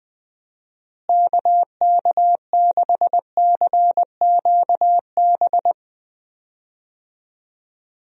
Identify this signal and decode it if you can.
Morse code, heard as a keyed tone, about 21 wpm, 700 Hz. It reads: KK6CQB
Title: USA Call Signs - 6 Characters - Repeated 2x - 20wpm